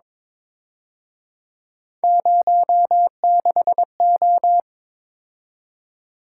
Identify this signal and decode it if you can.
Morse code, heard as a keyed tone, about 27 wpm, 700 Hz. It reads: 06O
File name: Speed-Racing - Three Letter-Number Combinations - 22wpm